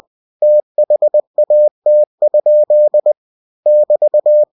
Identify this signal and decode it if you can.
Morse code, heard as a keyed tone, about 20 wpm, 600 Hz. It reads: THAT? <BT>
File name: cw-books / winnie-the-pooh / winnie-20wpm-page-0045